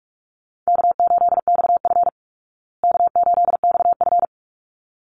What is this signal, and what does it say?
Morse code, heard as a keyed tone, about 44 wpm, 700 Hz. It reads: K8XF K8XF
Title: Call Signs Repeated 3x - 45wpm